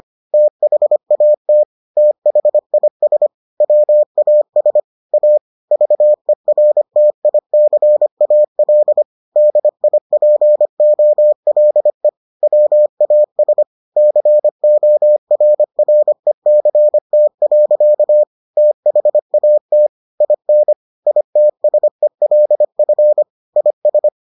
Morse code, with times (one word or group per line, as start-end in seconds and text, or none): 0.3-1.6 THAT
2.0-3.3 THIS
3.6-4.8 WAS
5.1-5.4 A
5.7-9.0 VERTICAL
9.4-12.1 DIPOLE
12.4-13.6 WAS
14.0-18.2 CORRECT.
18.6-19.9 THAT
20.2-20.7 IN
21.1-23.2 ITSELF
23.6-24.1 IS